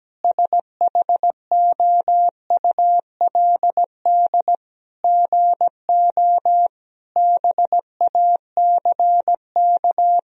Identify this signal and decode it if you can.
Morse code, heard as a keyed tone, about 17 wpm, 700 Hz. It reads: SHOULD GO BACK
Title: Sentences from Top 1000 Words - Review Part 4 - 17wpm